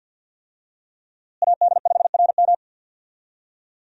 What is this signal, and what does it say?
Morse code, heard as a keyed tone, about 52 wpm, 700 Hz. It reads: AD5RM
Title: USA Call Signs - 5 Characters - Repeated 2x - 50wpm